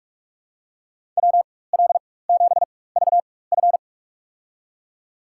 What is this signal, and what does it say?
Morse code, heard as a keyed tone, about 46 wpm, spaced as for 21 wpm, 700 Hz. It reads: WL7VF
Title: USA Call Signs - 5 Characters - 4x Character Spacing - 45wpm